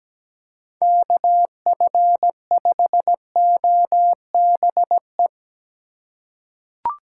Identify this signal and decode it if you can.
Morse code, heard as a keyed tone, about 17 wpm, 700 Hz. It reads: KF5OBE
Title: USA Call Signs - 6 Characters - 4x Character Spacing - 17wpm